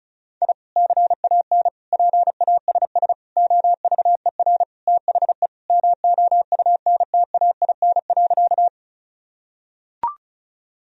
Morse code, heard as a keyed tone, about 36 wpm, 700 Hz. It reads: I CAN PASS OVER THE MOUNTAIN.
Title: Sentences from Top 1000 Words - Review Part 1 - 35wpm